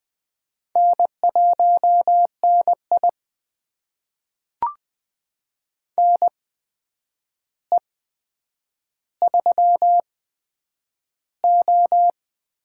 Morse code, heard as a keyed tone, about 20 wpm, spaced as for 9 wpm, 700 Hz. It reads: N1NI NE3O
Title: USA Call Signs - 4 Characters - 8x Character Spacing - 20wpm